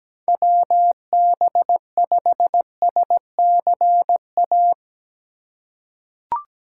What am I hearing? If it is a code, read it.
Morse code, heard as a keyed tone, about 17 wpm, 700 Hz. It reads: WB5SCA
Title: USA Call Signs - 3x Character Spacing - 17wpm